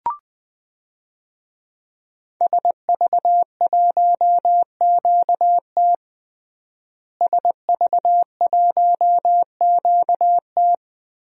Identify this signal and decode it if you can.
Morse code, heard as a keyed tone, about 20 wpm, 700 Hz. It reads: SV1QT SV1QT
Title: DXCC-100 Call Signs - 5 Characters - Repeated 3x - 20wpm